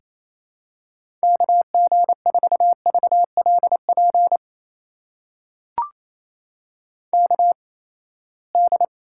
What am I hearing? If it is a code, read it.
Morse code, heard as a keyed tone, about 28 wpm, spaced as for 19 wpm, 700 Hz. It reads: KG4VLP KD
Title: USA Call Signs - 6 Characters - 8x Character Spacing - 28wpm